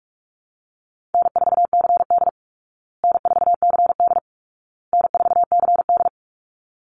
Morse code, heard as a keyed tone, about 44 wpm, 700 Hz. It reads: N4CD N4CD N4CD
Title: Call Signs Repeated 2x - 45wpm